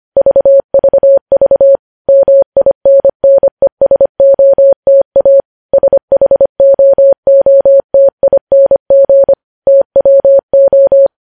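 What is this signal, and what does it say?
Morse code, heard as a keyed tone, about 25 wpm, 550 Hz. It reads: VVV MINNESOTA SHOOTING TWO
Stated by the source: News headlines sent in Morse code at 25.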